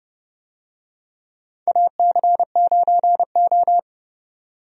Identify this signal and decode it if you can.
Morse code, heard as a keyed tone, about 30 wpm, 700 Hz. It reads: AC9O
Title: USA Call Signs - 4 Characters - 30wpm